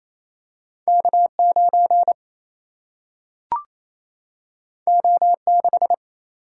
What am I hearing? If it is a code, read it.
Morse code, heard as a keyed tone, about 28 wpm, 700 Hz. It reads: K 9 O 6
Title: Single-Letter Single-Number 28wpm